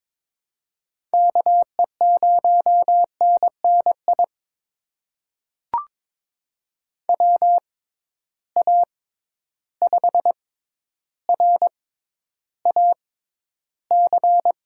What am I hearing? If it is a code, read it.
Morse code, heard as a keyed tone, about 23 wpm, spaced as for 11 wpm, 700 Hz. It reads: KE0NNI WA5RAC
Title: USA Call Signs - 6 Characters - 6x Character Spacing - 22wpm